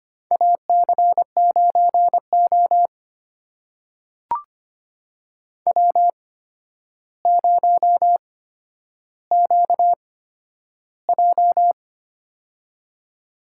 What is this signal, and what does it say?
Morse code, heard as a keyed tone, about 25 wpm, spaced as for 11 wpm, 700 Hz. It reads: AC9O W0QJ
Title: USA Call Signs - 4 Characters - 8x Character Spacing - 25wpm